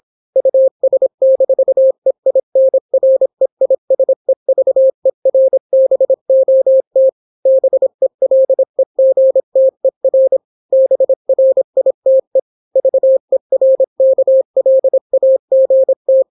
Morse code, heard as a keyed tone, about 26 wpm, 530 Hz. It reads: US-EINREISEVERBOT BELEGTER BRITE VERKLAGT